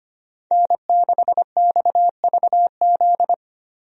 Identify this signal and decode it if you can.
Morse code, heard as a keyed tone, about 25 wpm, 700 Hz. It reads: N6XVZ
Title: USA Call Signs - 5 Characters - Repeated 3x - 25wpm